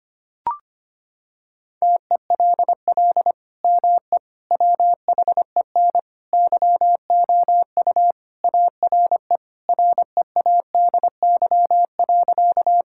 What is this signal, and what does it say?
Morse code, heard as a keyed tone, about 25 wpm, 700 Hz. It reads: TELL ME WHEN YOU ARE READY.
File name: practice / 25wpm / Sentences with 2-Word Phrases - 25wpm